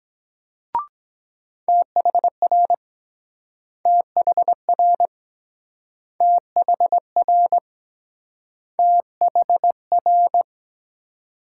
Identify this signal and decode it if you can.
Morse code, heard as a keyed tone, about 22 wpm, 700 Hz. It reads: THR THR THR THR